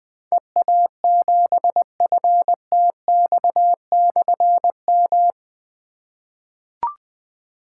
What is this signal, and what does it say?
Morse code, heard as a keyed tone, about 20 wpm, 700 Hz. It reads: EA7FTX/M
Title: Difficult Call Signs 20wpm